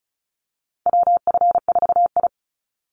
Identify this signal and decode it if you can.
Morse code, heard as a keyed tone, about 34 wpm, 700 Hz. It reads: WF4I
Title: NA SOTA Chaser Callsigns 35wpm